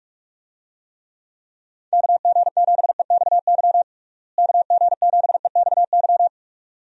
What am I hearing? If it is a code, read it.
Morse code, heard as a keyed tone, about 43 wpm, 700 Hz. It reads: KG7EXY KG7EXY
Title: USA Call Signs - 6 Characters - Repeated 2x - 45wpm